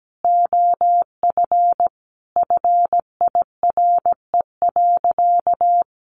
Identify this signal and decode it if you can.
Morse code, heard as a keyed tone, about 17 wpm, 700 Hz. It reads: OF FIRE.
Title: Easy Phrases 17wpm